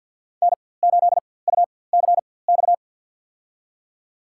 Morse code, heard as a keyed tone, about 53 wpm, spaced as for 24 wpm, 690 Hz. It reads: N8UCX
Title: USA Call Signs - 4x Character Spacing - 50wpm